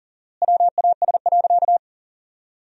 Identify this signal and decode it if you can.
Morse code, heard as a keyed tone, about 40 wpm, 700 Hz. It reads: WAS.
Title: 4x Word-Spacing - Sentences from Top 300 Words - 40wpm